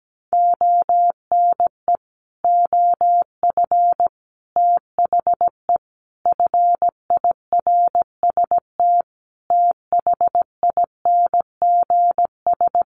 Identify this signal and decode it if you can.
Morse code, heard as a keyed tone, about 17 wpm, 700 Hz. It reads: ONE OF THE FIRST THINGS
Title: Sets of 5 Words Encore 17wpm